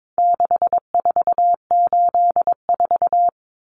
Morse code, heard as a keyed tone, about 22 wpm, 700 Hz. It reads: 6484
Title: Numbers - 4-Digit 22wpm